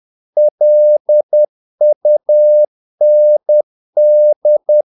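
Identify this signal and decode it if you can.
Morse code, heard as a keyed tone, about 10 wpm, 600 Hz. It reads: LUND
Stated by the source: Morse practice of short Danish words at 10 wpm, from CW med Gnister.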